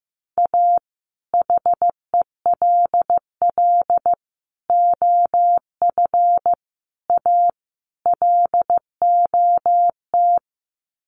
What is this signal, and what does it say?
Morse code, heard as a keyed tone, about 14 wpm, 700 Hz. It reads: A HELL OF A LOT
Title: Sets of 5 Words Encore 15wpm